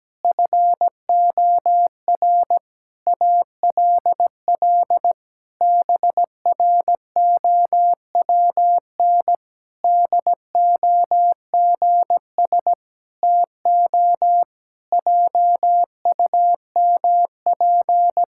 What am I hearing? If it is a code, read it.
Morse code, heard as a keyed tone, about 17 wpm, 700 Hz. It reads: FOR ALL BROWN DOGS TO JUMP
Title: Pangram Challenge v2 - 17wpm